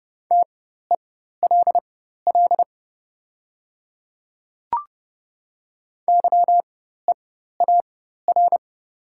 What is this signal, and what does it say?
Morse code, heard as a keyed tone, about 30 wpm, spaced as for 14 wpm, 690 Hz. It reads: TELL YEAR
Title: 4x Character Spacing - Top 300 Words - 30wpm